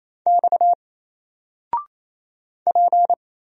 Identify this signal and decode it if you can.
Morse code, heard as a keyed tone, about 28 wpm, 700 Hz. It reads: X P
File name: Single-Letter-Number-28wpm